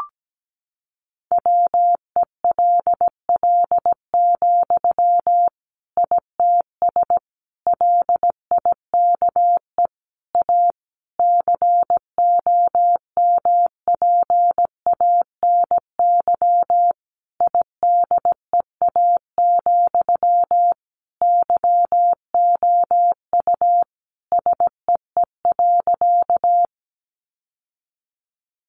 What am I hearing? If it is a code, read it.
Morse code, heard as a keyed tone, about 17 wpm, 700 Hz. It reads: WELL, ITS LIKE A COMPANY IDEA, YOU SEE.
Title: Sentences from Top 600 Words 17wpm